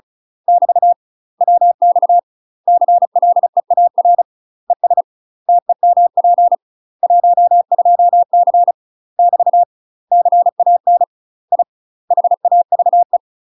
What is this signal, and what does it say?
Morse code, heard as a keyed tone, about 34 wpm, 700 Hz. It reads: <BT> WX CLEAR ES TEMP 12C <BT> CAN I HAVE
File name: UK MASTER QSOs TRAINING / 35WPM / UK MASTER QSOs 0017